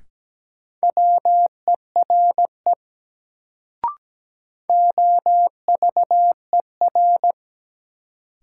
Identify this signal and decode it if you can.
Morse code, heard as a keyed tone, about 17 wpm, 700 Hz. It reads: WERE OVER
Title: Sentences from Top 100 Words - New Word Drill - 17wpm